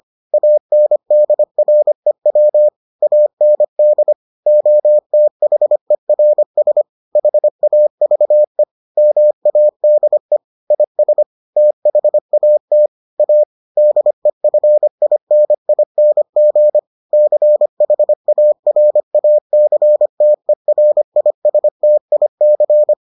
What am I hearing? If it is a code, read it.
Morse code, heard as a keyed tone, about 25 wpm, 600 Hz. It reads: ANDREW AND OTHERS HAVE MADE IS THAT A DEFINING CHARACTERISTIC